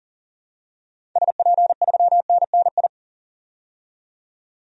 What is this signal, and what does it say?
Morse code, heard as a keyed tone, about 40 wpm, 700 Hz. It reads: SP3NNI
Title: DXCC-100 Call Signs - 6 Characters - 40wpm